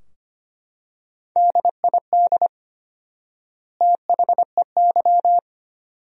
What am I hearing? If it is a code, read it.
Morse code, heard as a keyed tone, about 25 wpm, 700 Hz. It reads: DID THEY
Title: 4x Word-Spacing - Sentences from Top 400 Words - 25wpm